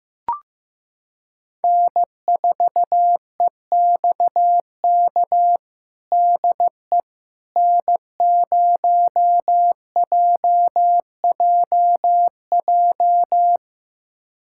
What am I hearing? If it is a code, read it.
Morse code, heard as a keyed tone, about 15 wpm, 700 Hz. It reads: N4EXK DE N0JJJ
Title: USA Callsign DE Callsign - 5 Characters - 15wpm